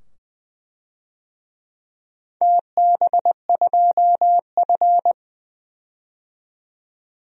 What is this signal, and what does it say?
Morse code, heard as a keyed tone, about 20 wpm, 700 Hz. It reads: TB2F
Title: DXCC-100 Call Signs - 4 Characters - 20wpm